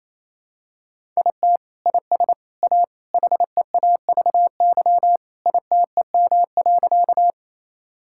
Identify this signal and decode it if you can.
Morse code, heard as a keyed tone, about 28 wpm, 700 Hz. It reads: IT IS A HEAVY ITEM.
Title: Sentences from Top 1000 Words - Review Part 6 - 28wpm